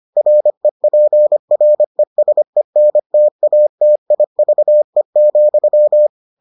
Morse code, heard as a keyed tone, about 25 wpm, 600 Hz. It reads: REPRESENTATIVE,